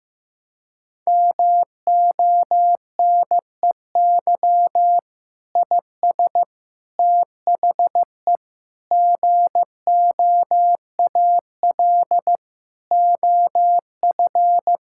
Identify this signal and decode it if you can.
Morse code, heard as a keyed tone, about 15 wpm, 700 Hz. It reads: MONEY IS THE GOAL OF
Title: Sentences from Top 1000 Words - Review Part 5 - 15wpm